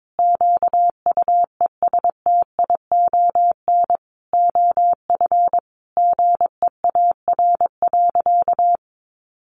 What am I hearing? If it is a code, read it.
Morse code, heard as a keyed tone, about 22 wpm, 700 Hz. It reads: QUESTION OF GEAR.
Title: Sentences from Top 1400 Words 22wpm